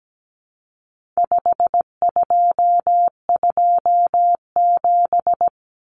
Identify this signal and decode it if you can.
Morse code, heard as a keyed tone, about 17 wpm, 700 Hz. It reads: 5227
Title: Common Cut Numbers - 4-Digit 17wpm